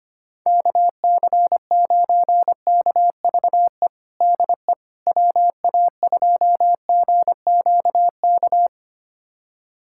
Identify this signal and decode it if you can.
Morse code, heard as a keyed tone, about 25 wpm, 700 Hz. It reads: KC9KVE DE WA2GQK
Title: USA Callsign DE Callsign - 6 Characters - 25wpm